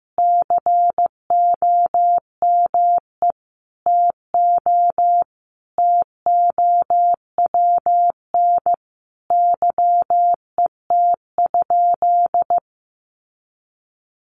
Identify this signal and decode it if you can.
Morse code, heard as a keyed tone, about 15 wpm, 700 Hz. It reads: COME TO TOWN YET?
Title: Sentences from Top 1500 Words 15wpm